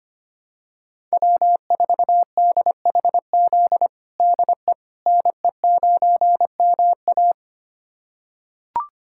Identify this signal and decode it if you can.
Morse code, heard as a keyed tone, about 25 wpm, 700 Hz. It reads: W4DHZ DE NE9MA
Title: USA Callsign DE Callsign - 5 Characters - 25wpm